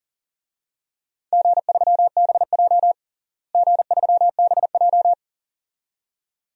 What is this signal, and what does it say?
Morse code, heard as a keyed tone, about 40 wpm, 700 Hz. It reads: G3BJ G3BJ
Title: DXCC-100 Call Signs - 4 Characters - Repeated 2x - 40wpm